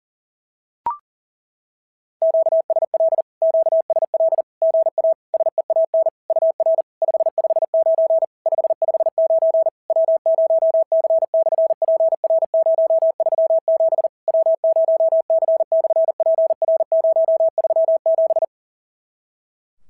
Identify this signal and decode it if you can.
Morse code, heard as a keyed tone, about 40 wpm, 650 Hz. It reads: QSL QSL GA SEAN UR 559 559 W0C/PR037 W0C/PR037
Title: SOTA QSO Practice 40wpm